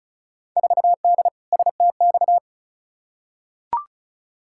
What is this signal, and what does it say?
Morse code, heard as a keyed tone, about 34 wpm, 700 Hz. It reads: 4D STX
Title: Field Day Exchanges Encore - 35wpm